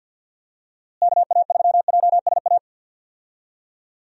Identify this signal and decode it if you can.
Morse code, heard as a keyed tone, about 51 wpm, 700 Hz. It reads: KA3JSA
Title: USA Call Signs - 6 Characters - Repeated 3x - 50wpm